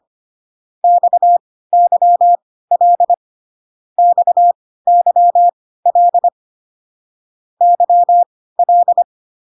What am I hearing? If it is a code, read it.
Morse code, heard as a keyed tone, about 25 wpm, spaced as for 16 wpm, 700 Hz. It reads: XYL XYL YL